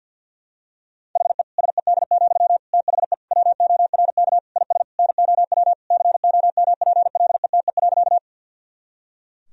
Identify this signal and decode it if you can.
Morse code, heard as a keyed tone, about 50 wpm, 700 Hz. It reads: HE SED, THE WORK IS NOW COMPLETE.
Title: Sentences from Top 400 Words and Abbreviations 50wpm